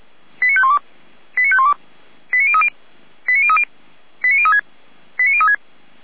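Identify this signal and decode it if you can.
Five-tone selective call (ZVEI), heard as quick runs of five tones, tones of about 70 ms. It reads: 89512 89512 89030 89030 89036 89036
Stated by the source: ZVEI five-tone selective call 89140 received by SDR, logged on the feuerwehr-orlamuende.org audio server.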